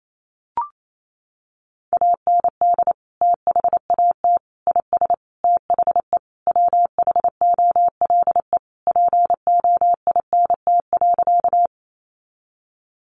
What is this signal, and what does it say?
Morse code, heard as a keyed tone, about 28 wpm, 700 Hz. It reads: AND THAT IS THE WHOLE POINT.
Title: Sentences from Top 400 Words 28wpm